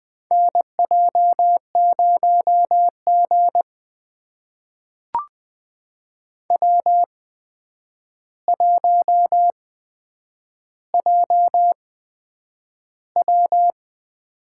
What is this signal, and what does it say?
Morse code, heard as a keyed tone, about 20 wpm, spaced as for 9 wpm, 700 Hz. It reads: NJ0G W1JW
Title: USA Call Signs - 4 Characters - 8x Character Spacing - 20wpm